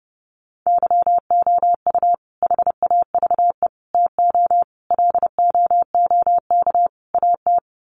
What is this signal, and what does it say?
Morse code, heard as a keyed tone, about 30 wpm, 700 Hz. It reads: YOU HAVE TO LOOK AT
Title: Sets of 5 Words Encore 30wpm